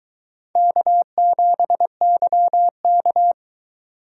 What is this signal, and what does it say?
Morse code, heard as a keyed tone, about 22 wpm, 700 Hz. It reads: K7YK